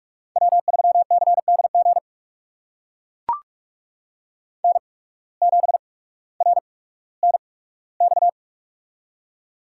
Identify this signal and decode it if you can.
Morse code, heard as a keyed tone, about 43 wpm, spaced as for 19 wpm, 700 Hz. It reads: W3CDG N7RNX
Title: USA Call Signs - 8x Character Spacing - 45wpm